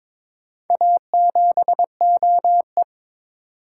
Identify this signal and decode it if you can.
Morse code, heard as a keyed tone, about 21 wpm, 700 Hz. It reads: A7OE
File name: practice / 22wpm / Four Letter-Number Combinations - 22wpm